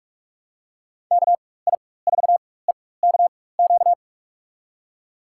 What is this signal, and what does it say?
Morse code, heard as a keyed tone, about 46 wpm, spaced as for 21 wpm, 700 Hz. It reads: KI4EKQ
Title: USA Call Signs - 4x Character Spacing - 45wpm